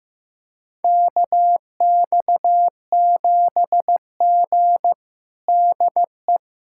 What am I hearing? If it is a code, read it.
Morse code, heard as a keyed tone, about 15 wpm, 700 Hz. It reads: KX7G DE
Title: USA Callsign DE Callsign - 4 Characters - 15wpm